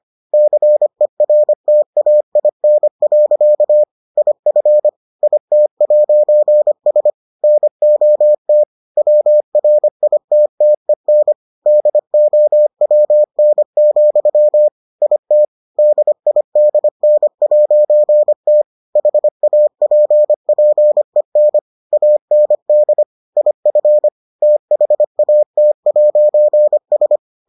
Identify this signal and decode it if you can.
Morse code, heard as a keyed tone, about 25 wpm, 600 Hz. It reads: CERTAIN. IF IT'S NOT WRITTEN DOWN, IT DIDN'T HAPPEN AND IF THAT'S